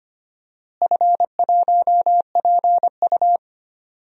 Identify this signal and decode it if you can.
Morse code, heard as a keyed tone, about 25 wpm, 700 Hz. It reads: F1PU
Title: DXCC-100 Call Signs - 4 Characters - Repeated 3x - 25wpm